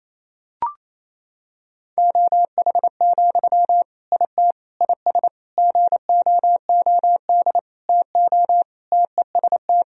Morse code, heard as a keyed tone, about 28 wpm, 700 Hz. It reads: OH, IT IS GOOD TO TEST
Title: Sentences from Top 1000 Words - Review Part 3 - 28wpm